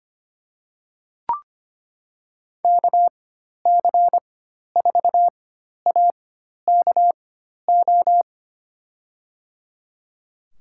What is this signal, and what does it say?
Morse code, heard as a keyed tone, about 25 wpm, spaced as for 12 wpm, 700 Hz. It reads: KC4AKO
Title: USA Call Signs - 6 Characters - 4x Character Spacing - 25wpm